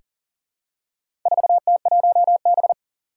Morse code, heard as a keyed tone, about 38 wpm, 700 Hz. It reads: 4T1B
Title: Four Challenging Letter-Number Combinations 40wpm